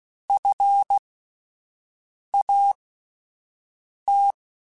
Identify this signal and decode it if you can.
Morse code, heard as a keyed tone, about 16 wpm, spaced as for 5 wpm, 800 Hz. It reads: FAT
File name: Alphabetical / Alphabetical mp3s / Fs @5